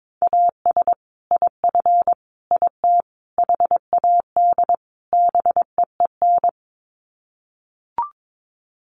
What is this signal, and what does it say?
Morse code, heard as a keyed tone, about 22 wpm, 700 Hz. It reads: AS IF IT HAD BEEN
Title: Sets of 5 Words Encore2 22wpm